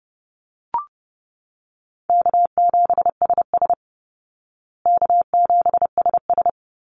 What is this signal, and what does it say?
Morse code, heard as a keyed tone, about 30 wpm, 700 Hz. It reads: K7SS K7SS
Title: Easy Call Signs Repeated 2x - 30wpm